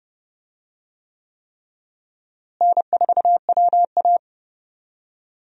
Morse code, heard as a keyed tone, about 30 wpm, 700 Hz. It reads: N4WA
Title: USA Call Signs - 4 Characters - Repeated 3x - 30wpm